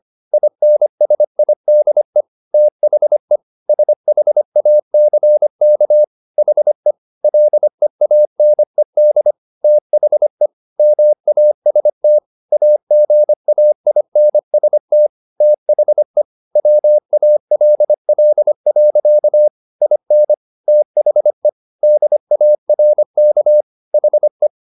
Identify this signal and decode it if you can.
Morse code, heard as a keyed tone, about 25 wpm, 600 Hz. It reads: INSIDE THE SHACK HE LEANED THE MAST AGAINST THE WALL. IN THE DARK HE